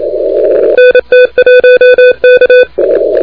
Morse code, heard as a keyed tone, about 27 wpm, 2500 Hz, with noise. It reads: NT1K
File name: recordings / PA3EVY de NT1K ARRL10M CW 28MHz